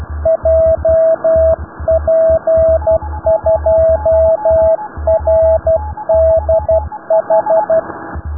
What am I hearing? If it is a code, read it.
Morse code, heard as a keyed tone, about 12 wpm, 650 Hz, with radio noise. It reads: JP2RDH
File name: rubberst01 60 noise suugi